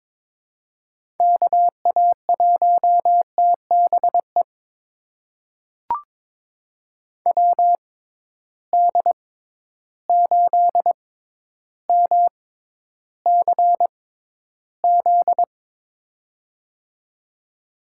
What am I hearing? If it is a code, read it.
Morse code, heard as a keyed tone, about 22 wpm, spaced as for 11 wpm, 700 Hz. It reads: KA1TBE WD8MCZ
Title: USA Call Signs - 6 Characters - 6x Character Spacing - 22wpm